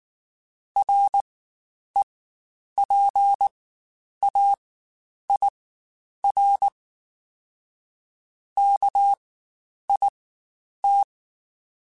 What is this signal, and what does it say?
Morse code, heard as a keyed tone, about 19 wpm, spaced as for 9 wpm, 800 Hz. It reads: REPAIR KIT
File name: Quotes Spaced / Quotes Spaced-mp3s / Airtight sp @18